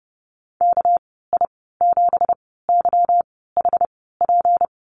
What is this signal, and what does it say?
Morse code, heard as a keyed tone, about 30 wpm, spaced as for 17 wpm, 700 Hz. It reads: KI7YHP
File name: Lesson 37g - Letter x - Call Signs 30wpm